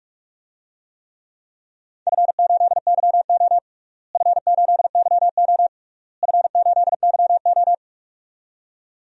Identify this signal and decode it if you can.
Morse code, heard as a keyed tone, about 44 wpm, 700 Hz. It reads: F8YO F8YO F8YO
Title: DXCC-100 Call Signs - 4 Characters - Repeated 3x - 45wpm